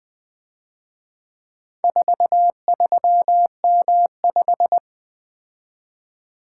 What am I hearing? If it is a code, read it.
Morse code, heard as a keyed tone, about 23 wpm, 700 Hz. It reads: 43M5
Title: Speed-Racing - Four Letter-Number Combinations - 20wpm